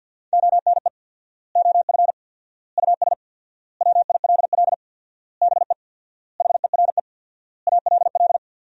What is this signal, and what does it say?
Morse code, heard as a keyed tone, about 50 wpm, 700 Hz. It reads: ONE OF US WILL BE HERE ALL
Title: 4x Word-Spacing - Sentences from Top 300 Words - 50wpm